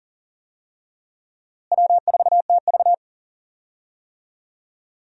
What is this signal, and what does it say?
Morse code, heard as a keyed tone, about 40 wpm, 700 Hz. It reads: W4TV
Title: USA Call Signs - 4 Characters - Repeated 2x - 40wpm